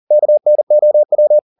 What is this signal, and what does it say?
Morse code, heard as a keyed tone, about 40 wpm, 600 Hz. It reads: KNOW
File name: the old man and the sea 2 40wpm